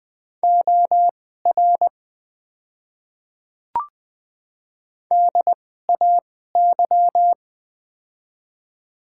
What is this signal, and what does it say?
Morse code, heard as a keyed tone, about 21 wpm, spaced as for 15 wpm, 700 Hz. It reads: OR DAY
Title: 2x Character Spacing - Top 100 Words - 20wpm